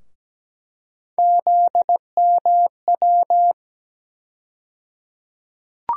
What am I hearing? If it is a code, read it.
Morse code, heard as a keyed tone, about 17 wpm, 700 Hz. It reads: ZMW
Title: USA Call Sign Postfixes - 3 Letters 17wpm